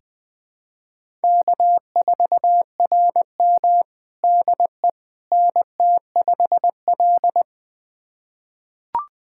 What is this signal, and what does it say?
Morse code, heard as a keyed tone, about 20 wpm, 700 Hz. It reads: K4RM DE NT5L
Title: USA Callsign DE Callsign - 4 Characters - 20wpm